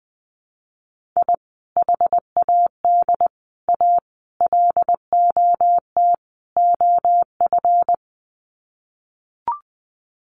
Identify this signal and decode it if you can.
Morse code, heard as a keyed tone, about 20 wpm, 700 Hz. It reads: I HAD A LOT OF
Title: Sets of 5 Words Encore 20wpm